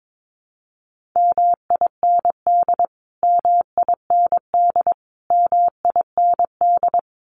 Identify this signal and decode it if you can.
Morse code, heard as a keyed tone, about 22 wpm, 700 Hz. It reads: MIND MIND MIND
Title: Top 500 Words Repeated 22wpm